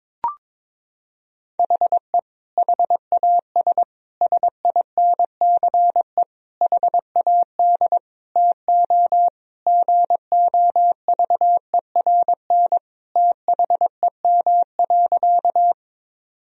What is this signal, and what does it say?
Morse code, heard as a keyed tone, about 22 wpm, 700 Hz. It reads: HE HAS SINCE HAD TO GOVERN THEM.
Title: Sentences from Top 1000 Words - Review Part 2 - 22wpm